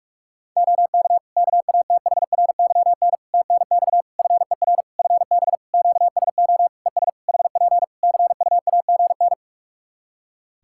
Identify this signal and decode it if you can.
Morse code, heard as a keyed tone, about 46 wpm, 700 Hz. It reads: OK KATHRYN TNX FER FB QSO ES HP CUAGN
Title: K1USN SST QSO Practice 45wpm